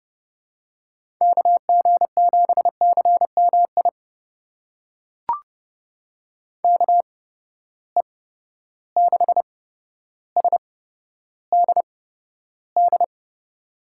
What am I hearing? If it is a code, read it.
Morse code, heard as a keyed tone, about 30 wpm, spaced as for 13 wpm, 700 Hz. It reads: KG7CMI KE6SDD
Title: USA Call Signs - 6 Characters - 8x Character Spacing - 30wpm